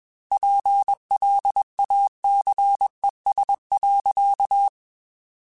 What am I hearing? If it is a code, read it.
Morse code, heard as a keyed tone, about 21 wpm, 800 Hz. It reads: PLACES.
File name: Chuckles / Chuckles-mp3s / MiddleAge @20